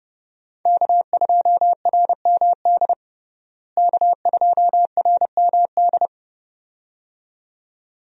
Morse code, heard as a keyed tone, about 30 wpm, 700 Hz. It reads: K2RMD K2RMD
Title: USA Call Signs - 5 Characters - Repeated 2x - 30wpm